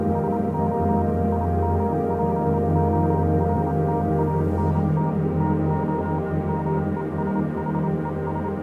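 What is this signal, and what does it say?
Morse code, heard as a keyed tone, about 27 wpm, 940 Hz, with noise. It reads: SO IMPORTANT TO MESHES
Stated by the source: Morse code version of a fan-made pony music track.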